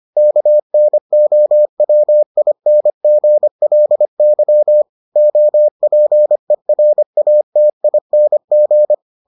Morse code, heard as a keyed tone, about 25 wpm, 600 Hz. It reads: KNOWINGLY OPERATING